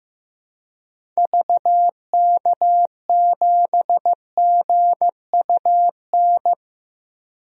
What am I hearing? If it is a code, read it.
Morse code, heard as a keyed tone, about 15 wpm, 700 Hz. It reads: VK7GUN
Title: DXCC-100 Call Signs - 6 Characters - 15wpm